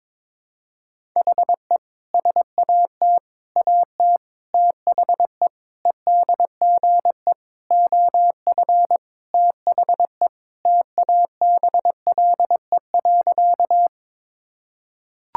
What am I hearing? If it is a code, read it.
Morse code, heard as a keyed tone, about 21 wpm, 700 Hz. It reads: HE SAT AT THE EDGE OF THE TABLE.
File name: practice / 22wpm / Sentences with 3-Word Phrases - 22wpm